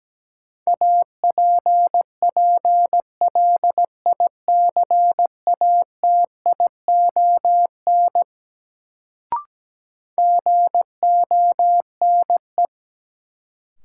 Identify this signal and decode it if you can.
Morse code, heard as a keyed tone, about 17 wpm, 700 Hz. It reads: APPLICATION GONE
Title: Sentences from Top 1200 Words - New Word Drill - 17wpm